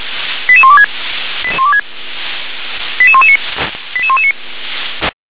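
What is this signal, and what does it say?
Five-tone selective call (ZVEI), heard as quick runs of five tones, tones of about 70 ms. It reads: 88126 88119 88119
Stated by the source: ZVEI five-tone selective call 88631 received by SDR.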